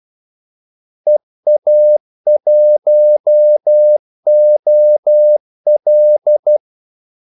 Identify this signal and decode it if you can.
Morse code, heard as a keyed tone, about 12 wpm, 600 Hz. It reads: EA1OL
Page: 2 ekstra ordmellemrum mellem kaldesignalerne